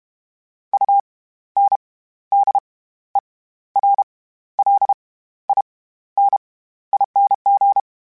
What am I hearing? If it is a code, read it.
Morse code, heard as a keyed tone, about 32 wpm, spaced as for 14 wpm, 800 Hz. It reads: UNDERLINING
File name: Special INGs / Special INGs-mp3s / SlowToFaster ing @30